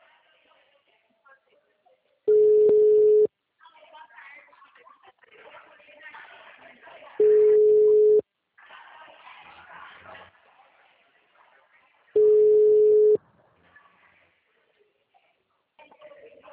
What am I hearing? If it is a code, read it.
Ringback tone (Europe, 425 Hz), with speech.
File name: out- -2016- - - .16304